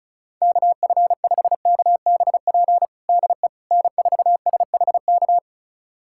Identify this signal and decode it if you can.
Morse code, heard as a keyed tone, about 34 wpm, 700 Hz. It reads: KF5KBP DE N4SHK
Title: USA Callsign DE Callsign 35wpm